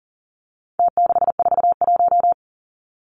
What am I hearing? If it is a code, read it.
Morse code, heard as a keyed tone, about 40 wpm, 700 Hz. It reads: T641
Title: Numbers - Contest Style - 4-Digit 40wpm